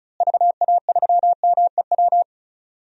Morse code, heard as a keyed tone, about 36 wpm, 700 Hz. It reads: VA3MEW
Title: DXCC-100 Call Signs - 6 Characters - Repeated 3x - 35wpm